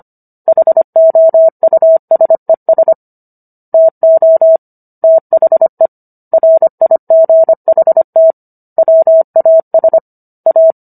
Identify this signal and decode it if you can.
Morse code, heard as a keyed tone, about 25 wpm, 650 Hz. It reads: HOUSES TO THE RIGHT WAS A